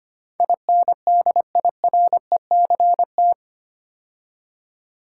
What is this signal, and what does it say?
Morse code, heard as a keyed tone, about 25 wpm, 700 Hz. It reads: INDIRECT
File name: 8 Letter Words - First 3 Letters Given - 25wpm